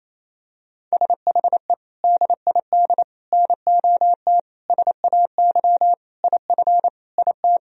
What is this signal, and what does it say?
Morse code, heard as a keyed tone, about 28 wpm, 700 Hz. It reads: SHE DID NOT SAY IF IT
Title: Sentences from Top 1000 Words - Review Part 3 - 28wpm